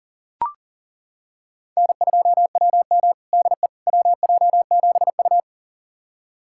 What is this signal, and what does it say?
Morse code, heard as a keyed tone, about 37 wpm, 700 Hz. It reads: N2WM DE WJ7U
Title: USA Callsign DE Callsign - 4 Characters - 40wpm